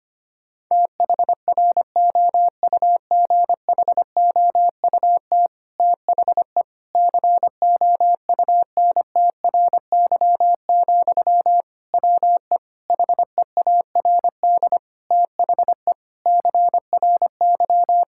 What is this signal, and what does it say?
Morse code, heard as a keyed tone, about 25 wpm, 700 Hz. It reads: THROUGHOUT THE COUNTRY, WE HEARD THE CRY
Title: Sentences from Top 1000 Words - Review Part 5 - 25wpm